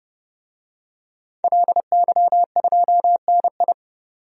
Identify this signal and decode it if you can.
Morse code, heard as a keyed tone, about 32 wpm, 700 Hz. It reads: LY2NI
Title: DXCC-100 Call Signs - 5 Characters - 30wpm